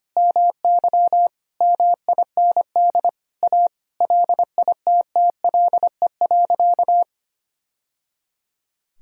Morse code, heard as a keyed tone, about 25 wpm, 700 Hz. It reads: MY MIND A LITTLE.
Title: Sentences from Top 1000 Words - Review Part 1 - 25wpm